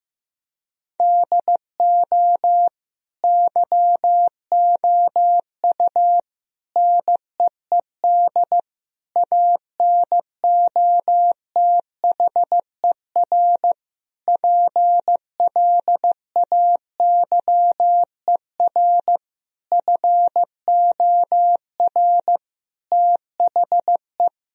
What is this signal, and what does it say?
Morse code, heard as a keyed tone, about 15 wpm, 700 Hz. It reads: DO YOU NEED ANOTHER PLAYER FOR THE
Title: Sentences from Top 1000 Words - Review Part 4 - 15wpm